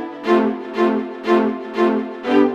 There is no code.